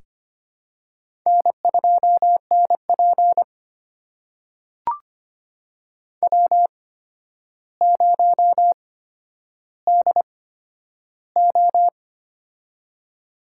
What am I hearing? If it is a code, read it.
Morse code, heard as a keyed tone, about 25 wpm, spaced as for 11 wpm, 700 Hz. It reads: N2NP W0DO
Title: USA Call Signs - 4 Characters - 8x Character Spacing - 25wpm